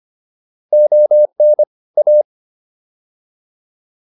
Morse code, heard as a keyed tone, about 25 wpm, 600 Hz. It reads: ON A